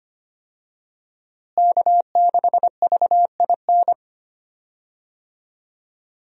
Morse code, heard as a keyed tone, about 25 wpm, 700 Hz. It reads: K6VIN
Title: USA Call Signs - 5 Characters - Repeated 3x - 25wpm